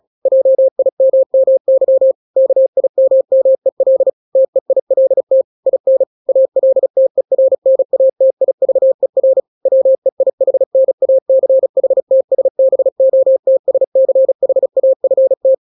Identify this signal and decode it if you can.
Morse code, heard as a keyed tone, about 36 wpm, 530 Hz. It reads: JIMMY KIMMEL TEILT IN ALTERNATIVER WEIHNACHTSBOTSCHAFT
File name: daily / 35wpm